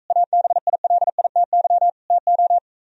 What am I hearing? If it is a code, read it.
Morse code, heard as a keyed tone, about 43 wpm, 700 Hz. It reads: ABILITY TO